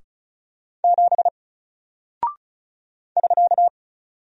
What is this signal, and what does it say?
Morse code, heard as a keyed tone, about 36 wpm, 700 Hz. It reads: 7 <SK>
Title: Single Character (Letter:Number:Punctuation:Prosign) Alt 35wpm